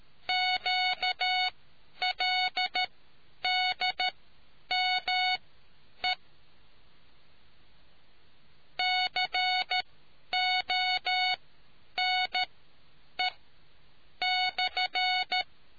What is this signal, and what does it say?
Morse code, heard as a keyed tone, about 13 wpm, spaced as for 9 wpm, 740 Hz. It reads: QLDME CONE/